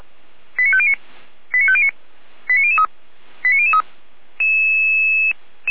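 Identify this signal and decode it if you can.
Five-tone selective call (ZVEI), heard as quick runs of five tones, tones of about 70 ms. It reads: 89509 89509 89003 89003
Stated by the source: ZVEI five-tone selective call received by SDR.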